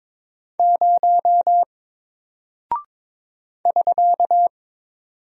Morse code, heard as a keyed tone, about 22 wpm, 700 Hz. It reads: 0 <SK>
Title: Single Character (Letter:Number:Punctuation:Prosign) Alt 22wpm